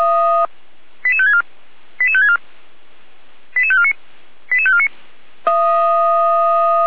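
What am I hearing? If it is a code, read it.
Five-tone selective call (ZVEI), heard as quick runs of five tones, tones of about 70 ms. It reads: 88564 88564 88549 88549
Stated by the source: Five-tone ZVEI selective call 88533 received by SDR, served by Feuerwehr Orlamünde.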